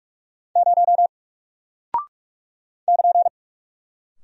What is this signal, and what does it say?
Morse code, heard as a keyed tone, about 44 wpm, 700 Hz. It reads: 0 <KN>
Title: Single Character (Letter:Number:Punctuation:Prosign) Alt 45wpm